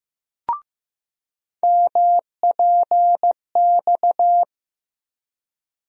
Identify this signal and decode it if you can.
Morse code, heard as a keyed tone, about 15 wpm, 700 Hz. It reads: MPX
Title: USA Call Sign Postfixes - 3 Letters 15wpm